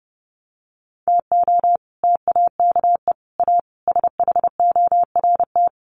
Morse code, heard as a keyed tone, about 30 wpm, 700 Hz. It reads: TO TAKE A SHORT
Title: Sets of 5 Words Encore2 30wpm